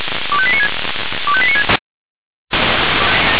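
Five-tone selective call (ZVEI), heard as quick runs of five tones, tones of about 70 ms. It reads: 25896 25896
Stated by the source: ZVEI five-tone selective call received by SDR.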